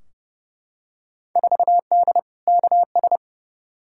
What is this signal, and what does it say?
Morse code, heard as a keyed tone, about 32 wpm, 700 Hz. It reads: 4D KS
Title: Field Day Exchanges Encore - 30wpm